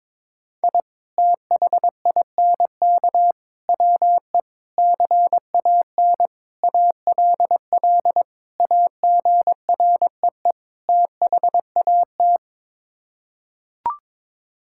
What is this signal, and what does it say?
Morse code, heard as a keyed tone, about 21 wpm, 700 Hz. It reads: I THINK WE CAN ALL AGREE THAT
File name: practice / 22wpm / Sets of 7 Words - 22wpm